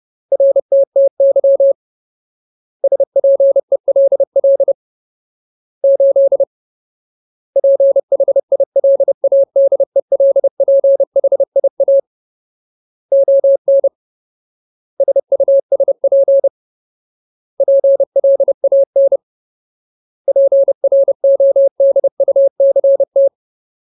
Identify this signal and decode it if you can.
Morse code, heard as a keyed tone, about 30 wpm, 550 Hz. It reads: RTTY SPELL 8 PHILADELPHIA ON SUSP PLAN PRODUCT
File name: Ultimate Word List / 30WPM Ultimate